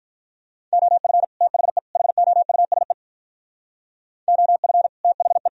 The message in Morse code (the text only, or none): OF THE HOUSE OF THE